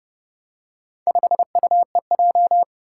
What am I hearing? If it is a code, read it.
Morse code, heard as a keyed tone, about 30 wpm, 700 Hz. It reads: 5UEJ
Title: Four Challenging Letter-Number Combinations 30wpm